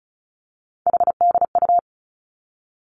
Morse code, heard as a keyed tone, about 36 wpm, 700 Hz. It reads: HDU
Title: Three Challenging Letter-Number Combinations 35wpm